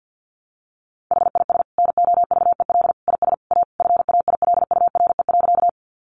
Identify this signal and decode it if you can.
Morse code, heard as a keyed tone, about 50 wpm, 700 Hz. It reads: HIS NOVEL IS A FAILURE.
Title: Sentences from Top 1300 Words 50wpm